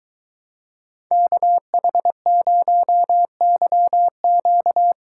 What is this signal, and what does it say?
Morse code, heard as a keyed tone, about 22 wpm, 700 Hz. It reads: KH0YQ